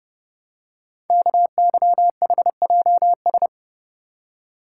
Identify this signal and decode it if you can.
Morse code, heard as a keyed tone, about 30 wpm, 700 Hz. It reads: KYHJS
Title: Code Groups - 5 Letters - 30wpm